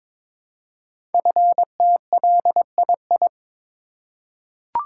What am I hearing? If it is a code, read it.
Morse code, heard as a keyed tone, about 23 wpm, 700 Hz. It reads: FTLII
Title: Code Groups - 5 Letters - 22wpm